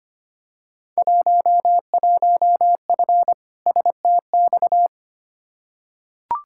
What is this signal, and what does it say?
Morse code, heard as a keyed tone, about 25 wpm, 700 Hz. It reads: 11F STX
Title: Field Day Exchanges Encore - 25wpm